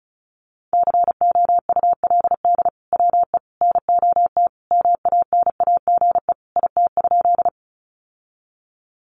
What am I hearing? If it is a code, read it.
Morse code, heard as a keyed tone, about 34 wpm, 700 Hz. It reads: COULD WE NOT MANAGE IT?
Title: Sentences from Top 1400 Words 35wpm